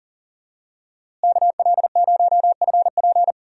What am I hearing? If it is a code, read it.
Morse code, heard as a keyed tone, about 40 wpm, 700 Hz. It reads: KL0FP
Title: USA Call Signs - 5 Characters - Repeated 2x - 40wpm